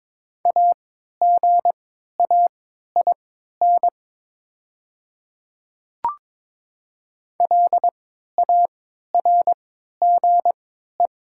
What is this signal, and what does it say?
Morse code, heard as a keyed tone, about 22 wpm, spaced as for 12 wpm, 690 Hz. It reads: AGAIN LARGE
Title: 3x Character Spacing - Top 200 Words - 22wpm